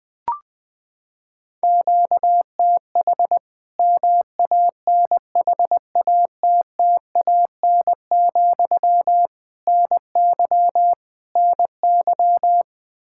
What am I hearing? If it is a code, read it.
Morse code, heard as a keyed tone, about 20 wpm, 700 Hz. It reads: QTH MANHATTAN, NY NY
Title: Practice QSO - Exchange 1 - Beginning - 20wpm